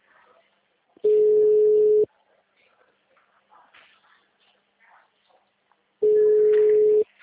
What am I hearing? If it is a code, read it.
Ringback tone (Europe, 425 Hz), with speech.